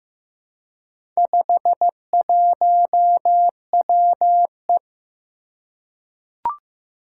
Morse code, heard as a keyed tone, about 15 wpm, 700 Hz. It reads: 51WE
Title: Four Challenging Letter-Number Combinations 15wpm